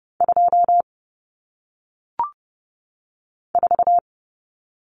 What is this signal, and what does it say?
Morse code, heard as a keyed tone, about 30 wpm, 700 Hz. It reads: 2 4
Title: Numbers - 1-Digit 30wpm